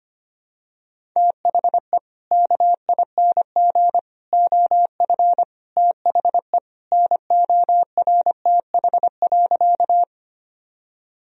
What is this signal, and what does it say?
Morse code, heard as a keyed tone, about 25 wpm, 700 Hz. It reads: THE KING OF THE NORTH.
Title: Sentences from Top 1000 Words - Review Part 1 - 25wpm